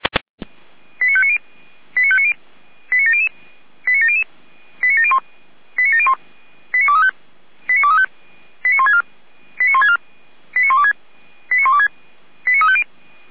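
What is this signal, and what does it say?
Five-tone selective call (ZVEI), heard as quick runs of five tones, tones of about 70 ms. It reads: 89590 89590 89700 89700 89701 89701 89235 89164 89164 89126 89126 89350
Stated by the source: ZVEI five-tone call received with an SDR.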